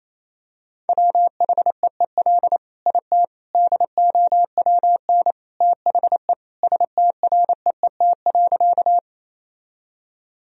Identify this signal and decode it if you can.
Morse code, heard as a keyed tone, about 27 wpm, 700 Hz. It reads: WHEEL IT DOWN THE STREET.
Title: Sentences from Top 1000 Words - Review Part 3 - 28wpm